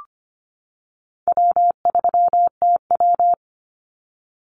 Morse code, heard as a keyed tone, about 25 wpm, 700 Hz. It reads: W3TW
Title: Easy Call Signs - 25wpm